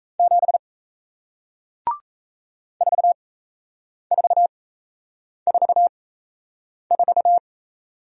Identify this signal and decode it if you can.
Morse code, heard as a keyed tone, about 35 wpm, spaced as for 8 wpm, 700 Hz. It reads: 7 4444